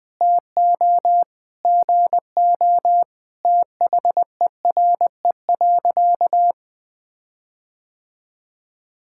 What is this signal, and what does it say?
Morse code, heard as a keyed tone, about 20 wpm, 700 Hz. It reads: TO GO THERE.
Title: Sentences with 5-Word Phrases - 20wpm